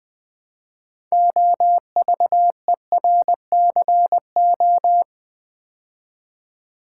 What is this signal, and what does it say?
Morse code, heard as a keyed tone, about 20 wpm, 700 Hz. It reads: OVERCO
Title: Build A Word - 2 Letters At A Time - 8-Letter Words - 20wpm